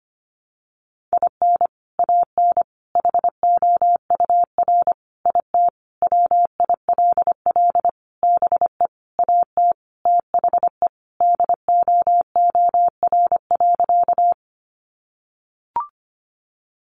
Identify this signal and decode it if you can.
Morse code, heard as a keyed tone, about 25 wpm, 700 Hz. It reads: IN AN HOUR IT WILL BE AT THE DOOR.
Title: Sentences from Top 400 Words 25wpm